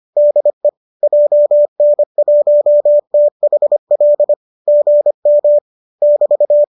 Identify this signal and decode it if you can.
Morse code, heard as a keyed tone, about 25 wpm, 590 Hz. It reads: DE JN1THL GM <BT>
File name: qso205 25